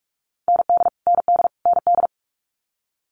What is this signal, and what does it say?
Morse code, heard as a keyed tone, about 46 wpm, 700 Hz. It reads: ND ND ND
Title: Multi-letter Phonemes Repeated 3x 45wpm